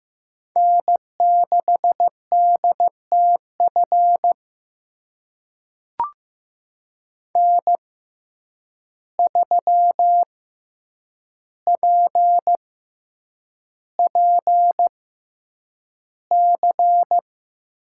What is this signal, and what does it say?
Morse code, heard as a keyed tone, about 15 wpm, spaced as for 8 wpm, 700 Hz. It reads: N6DTF N3PPC
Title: USA Call Signs - 5 Characters - 6x Character Spacing - 15wpm